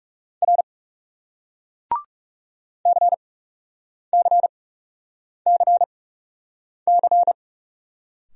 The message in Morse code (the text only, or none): R C C C C